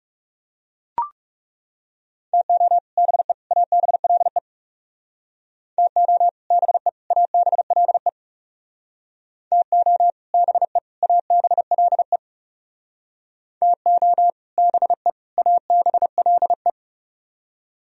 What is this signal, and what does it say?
Morse code, heard as a keyed tone, about 38 wpm, 700 Hz. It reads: TO BE ABLE TO BE ABLE TO BE ABLE TO BE ABLE